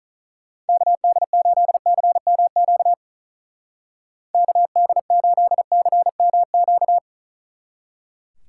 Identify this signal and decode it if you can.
Morse code, heard as a keyed tone, about 43 wpm, 700 Hz. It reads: KD8CMQ KD8CMQ
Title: Speed Racing - USA 2x3 Call Signs 35wpm